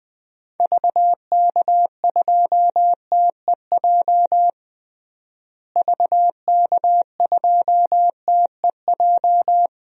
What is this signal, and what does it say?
Morse code, heard as a keyed tone, about 21 wpm, 700 Hz. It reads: VK2TEJ VK2TEJ
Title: DXCC-100 Call Signs - 6 Characters - Repeated 2x - 20wpm